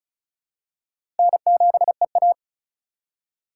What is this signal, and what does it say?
Morse code, heard as a keyed tone, about 34 wpm, 700 Hz. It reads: N7EA
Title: USA Call Signs - 4 Characters - 4x Character Spacing - 35wpm